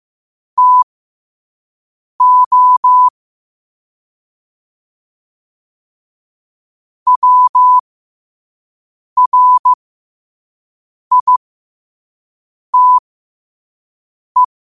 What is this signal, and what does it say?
Morse code, heard as a keyed tone, about 15 wpm, spaced as for 5 wpm, 1000 Hz. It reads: TO WRITE